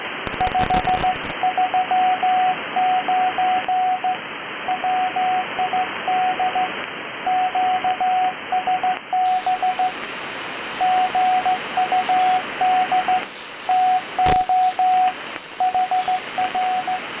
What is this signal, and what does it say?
Morse code, heard as a keyed tone, about 14 wpm, 740 Hz, with radio noise. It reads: 539 WID QSB GUD TO HR